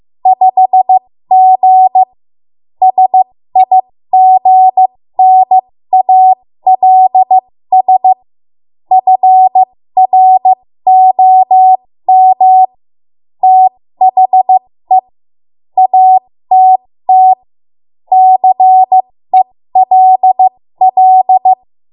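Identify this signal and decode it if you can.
Morse code, heard as a keyed tone, about 15 wpm, spaced as for 13 wpm, 750 Hz. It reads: 5G SIGNALS FROM THE ATT CELL
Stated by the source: 13 wpm Morse code practice archive transmission.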